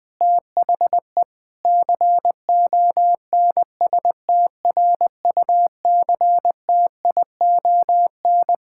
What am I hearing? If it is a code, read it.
Morse code, heard as a keyed tone, about 20 wpm, 700 Hz. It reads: THE CONSTRUCTION
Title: Sentences from Top 1600 Words 20wpm